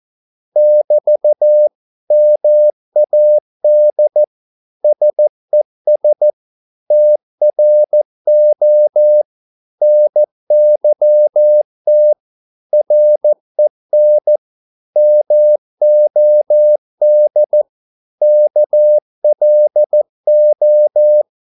Morse code, heard as a keyed tone, about 14 wpm, 600 Hz. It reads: <BT> MAD SES TRO NYT REN MOD KLO
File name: Korte ord DK 14wpm